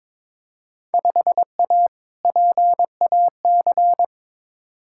Morse code, heard as a keyed tone, about 22 wpm, 700 Hz. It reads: 5A PAC
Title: Field Day Exchanges Encore - 22wpm